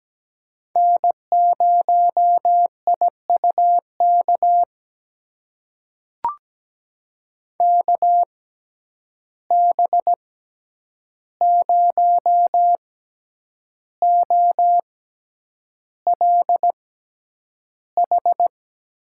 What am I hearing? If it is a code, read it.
Morse code, heard as a keyed tone, about 17 wpm, spaced as for 8 wpm, 700 Hz. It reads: N0IUK KB0OLH
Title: USA Call Signs - 6x Character Spacing - 17wpm